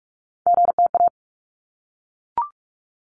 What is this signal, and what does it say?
Morse code, heard as a keyed tone, about 44 wpm, 700 Hz. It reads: GTA
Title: ARRL Sections 45wpm